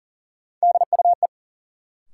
Morse code, heard as a keyed tone, about 40 wpm, 700 Hz. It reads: DUE
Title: Sets of 5 Words Encore - New Word Drill - 40wpm